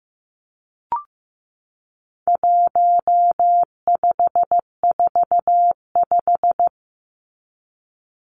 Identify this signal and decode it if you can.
Morse code, heard as a keyed tone, about 15 wpm, 700 Hz. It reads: 1545
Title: Numbers - 4-Digit 15wpm